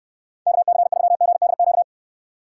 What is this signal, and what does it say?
Morse code, heard as a keyed tone, about 68 wpm, 700 Hz. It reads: KC3GDQ